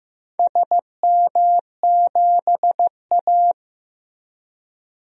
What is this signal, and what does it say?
Morse code, heard as a keyed tone, about 15 wpm, 700 Hz. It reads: SM7A
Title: DXCC-100 Call Signs - 4 Characters - Repeated 3x - 15wpm